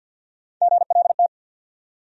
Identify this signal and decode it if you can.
Morse code, heard as a keyed tone, about 52 wpm, 700 Hz. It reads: GLT
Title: USA Call Sign Postfixes - 3 Letters 50wpm